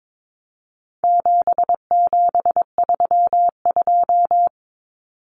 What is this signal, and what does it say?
Morse code, heard as a keyed tone, about 22 wpm, 700 Hz. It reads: 7732
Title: Common Cut Numbers - 4-Digit 22wpm